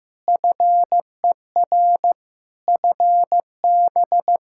Morse code, heard as a keyed tone, about 15 wpm, 700 Hz. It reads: FER FB
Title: Practice QSO - Exchange 4 - Ending - 15wpm